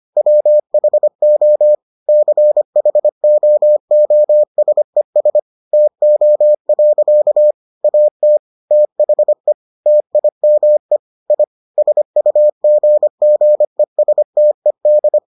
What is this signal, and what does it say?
Morse code, heard as a keyed tone, about 25 wpm, 600 Hz. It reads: WHO CHOOSES TO. AT THE TIME I SUGGESTED